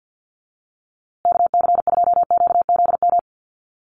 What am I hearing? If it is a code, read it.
Morse code, heard as a keyed tone, about 50 wpm, 700 Hz. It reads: KC2QZM
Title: USA Call Signs Repeated 2x 50wpm